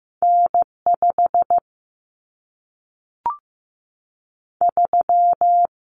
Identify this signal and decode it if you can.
Morse code, heard as a keyed tone, about 15 wpm, 700 Hz. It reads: N5 3
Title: Common Cut Numbers - 2-Digit 15wpm